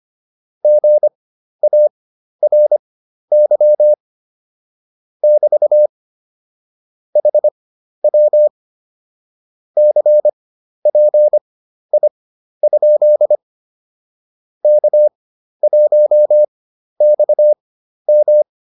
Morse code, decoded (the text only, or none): GARY <BT> HW CPI? K1XM